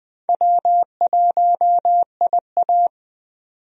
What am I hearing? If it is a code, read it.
Morse code, heard as a keyed tone, about 20 wpm, 700 Hz. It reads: W1IA
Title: USA Call Signs - 4 Characters - 20wpm